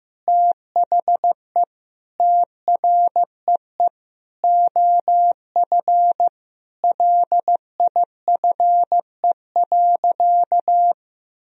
Morse code, heard as a keyed tone, about 15 wpm, 700 Hz. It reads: THE TREE OF LIFE.
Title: Sentences from Top 1000 Words - Review Part 1 - 15wpm